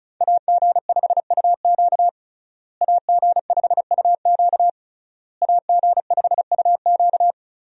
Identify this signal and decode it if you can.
Morse code, heard as a keyed tone, about 36 wpm, 700 Hz. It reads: AG5UQ AG5UQ AG5UQ
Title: USA Call Signs - 5 Characters - Repeated 3x - 35wpm